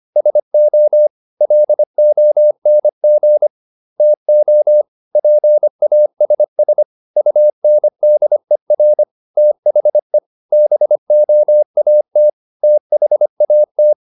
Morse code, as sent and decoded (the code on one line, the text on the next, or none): ... --- / .-.. --- -. --. / - --- / .--. .- ... ... / ..- -. -.. .-. / - .... / -... --- .- - / - .... .- -
SO LONG TO PASS UNDER THE BOAT THAT